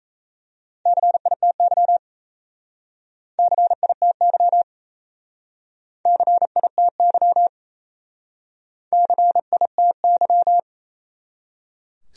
Morse code, heard as a keyed tone, about 36 wpm, 700 Hz. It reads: CITY CITY CITY CITY